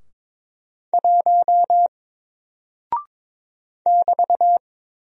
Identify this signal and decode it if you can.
Morse code, heard as a keyed tone, about 22 wpm, 700 Hz. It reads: 1 <BT>
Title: Single Character (Letter:Number:Punctuation:Prosign) Alt 22wpm